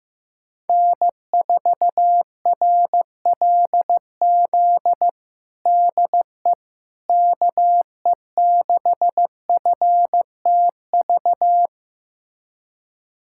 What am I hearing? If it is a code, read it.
Morse code, heard as a keyed tone, about 15 wpm, 700 Hz. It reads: N4RLZ DE KE6FTV
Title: USA Callsign DE Callsign 15wpm